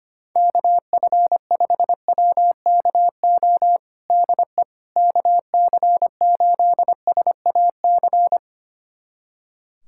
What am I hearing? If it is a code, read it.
Morse code, heard as a keyed tone, about 25 wpm, 700 Hz. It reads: KF5WKO DE KC8SAC
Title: USA Callsign DE Callsign 25wpm